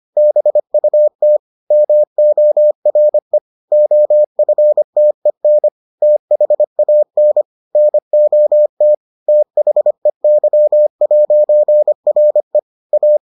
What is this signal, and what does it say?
Morse code, heard as a keyed tone, about 25 wpm, 600 Hz. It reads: BUT MORE OFTEN THAN NOT THEY'RE A